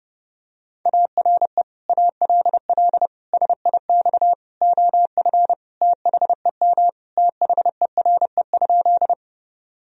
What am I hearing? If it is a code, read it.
Morse code, heard as a keyed tone, about 28 wpm, 700 Hz. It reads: ARE ALL SIX OF THEM THERE?
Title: Sentences from Top 1000 Words - Review Part 2 - 30wpm